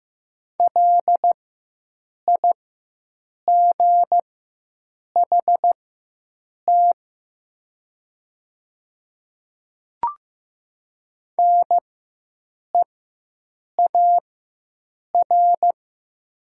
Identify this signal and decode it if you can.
Morse code, heard as a keyed tone, about 15 wpm, spaced as for 7 wpm, 700 Hz. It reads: LIGHT NEAR
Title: 4x Character Spacing - Top 300 Words - 15wpm